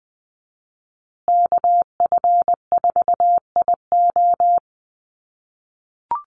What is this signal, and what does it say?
Morse code, heard as a keyed tone, about 20 wpm, 700 Hz. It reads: KF4IO
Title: USA Call Signs Repeated 2x 20wpm